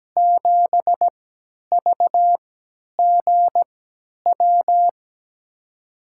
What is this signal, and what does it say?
Morse code, heard as a keyed tone, about 17 wpm, spaced as for 10 wpm, 700 Hz. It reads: 7 VGW
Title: USA Call Signs - 6 Characters - 3x Character Spacing - 17wpm